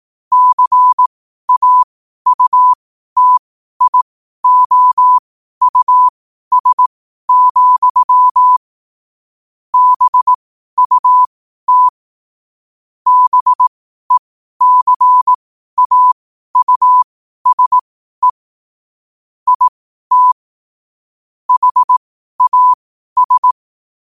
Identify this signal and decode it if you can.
Morse code, heard as a keyed tone, about 18 wpm, spaced as for 12 wpm, 1000 Hz. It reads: CAUTIOUS, BUT BECAUSE IT HAS
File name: qotdfiles / 12wpm